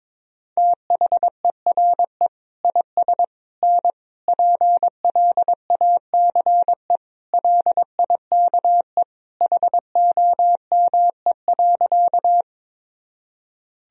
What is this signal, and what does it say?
Morse code, heard as a keyed tone, about 22 wpm, 700 Hz. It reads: THERE IS N PLACE LIKE HOME.
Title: Sentences from Top 200 Words and Abbreviations 22wpm